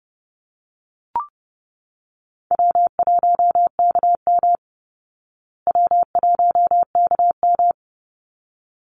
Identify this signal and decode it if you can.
Morse code, heard as a keyed tone, about 30 wpm, 700 Hz. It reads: W1KM W1KM
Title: Easy Call Signs Repeated 2x - 30wpm